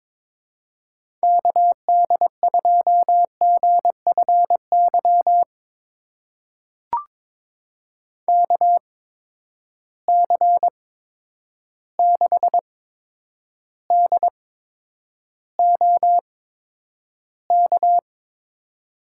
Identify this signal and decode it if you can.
Morse code, heard as a keyed tone, about 22 wpm, spaced as for 9 wpm, 700 Hz. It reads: KD2GFY KC6DOK
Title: USA Call Signs - 6 Characters - 8x Character Spacing - 22wpm